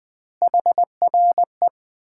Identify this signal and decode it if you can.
Morse code, heard as a keyed tone, about 20 wpm, 700 Hz. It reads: HRE
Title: Common QSO Phrases - 2x Word-Spacing - 20wpm